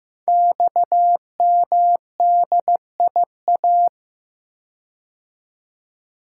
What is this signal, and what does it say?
Morse code, heard as a keyed tone, about 15 wpm, 700 Hz. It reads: XMDIA
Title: Code Groups - 5 Letters - 15wpm